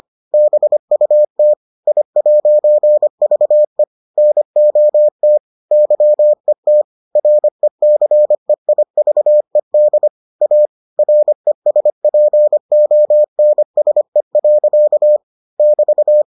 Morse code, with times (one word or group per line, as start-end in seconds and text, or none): 0.3-1.5 BUT
1.9-3.8 I'VE
4.2-5.4 NOT
5.7-6.8 YET
7.2-10.1 RECEIVED
10.4-10.7 A
11.0-15.2 RESPONSE.
15.6-16.2 <BT>